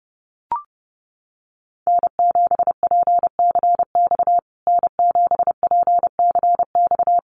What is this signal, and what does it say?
Morse code, heard as a keyed tone, about 30 wpm, 700 Hz. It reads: N7PCX N7PCX
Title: USA Call Signs Repeated 3x 30wpm